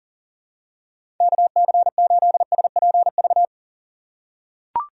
Morse code, heard as a keyed tone, about 40 wpm, 700 Hz. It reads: KC8SPV
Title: USA Call Signs - 6 Characters - 2x Character Spacing - 40wpm